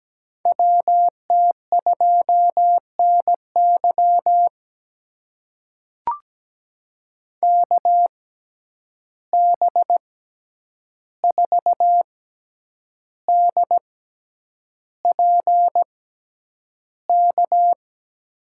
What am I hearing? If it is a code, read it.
Morse code, heard as a keyed tone, about 17 wpm, spaced as for 8 wpm, 700 Hz. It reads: WT2NY KB4DPK
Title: USA Call Signs - 6x Character Spacing - 17wpm